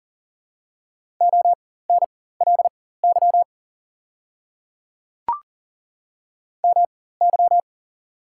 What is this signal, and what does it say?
Morse code, heard as a keyed tone, about 40 wpm, spaced as for 19 wpm, 690 Hz. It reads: ONLY MY
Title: 4x Character Spacing - Top 200 Words - 40wpm